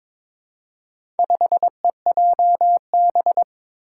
Morse code, heard as a keyed tone, about 22 wpm, 700 Hz. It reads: 5EJB
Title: Four Challenging Letter-Number Combinations 22wpm